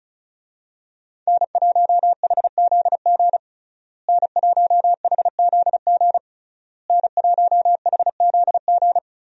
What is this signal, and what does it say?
Morse code, heard as a keyed tone, about 36 wpm, 700 Hz. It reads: N1HZG N1HZG N1HZG
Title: USA Call Signs - 5 Characters - Repeated 3x - 35wpm